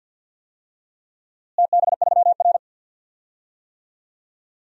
Morse code, heard as a keyed tone, about 50 wpm, 700 Hz. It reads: TB3R
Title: DXCC-100 Call Signs - 4 Characters - Repeated 3x - 50wpm